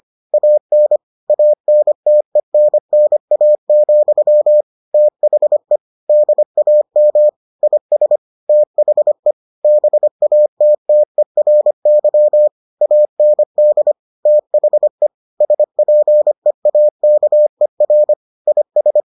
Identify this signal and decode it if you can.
Morse code, heard as a keyed tone, about 25 wpm, 600 Hz. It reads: AN ANTENNA, THE DAM IS THE BATTERY AND THE SPEAKER IS